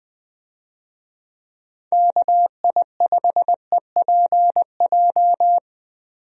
Morse code, heard as a keyed tone, about 20 wpm, 700 Hz. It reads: KI5EPJ
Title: USA Call Signs - 6 Characters - Repeated 2x - 20wpm